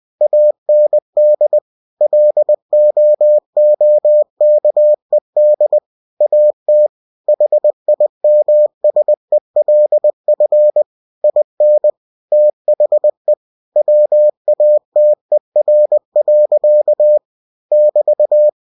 Morse code, heard as a keyed tone, about 20 wpm, 600 Hz. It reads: AND LOOKED AT HIMSELF IN THE WATER. <BT>
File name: cw-books / winnie-the-pooh / winnie-20wpm-page-0044